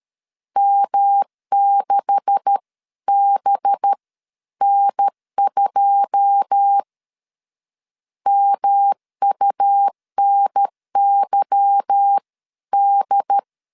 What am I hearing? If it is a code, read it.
Morse code, heard as a keyed tone, about 12 wpm, spaced as for 11 wpm, 800 Hz. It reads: M6BN2 MUNYD
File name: LEZIONI / 10 SIGNS / 10C DOT 12wpm 2451